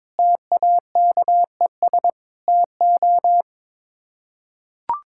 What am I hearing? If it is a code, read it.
Morse code, heard as a keyed tone, about 22 wpm, 700 Hz. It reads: TAKES TO
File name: practice / 22wpm / Sets of 7 Words - 22wpm